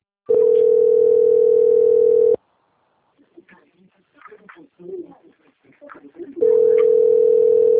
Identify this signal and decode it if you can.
Ringback tone (North America), 440 + 480 Hz.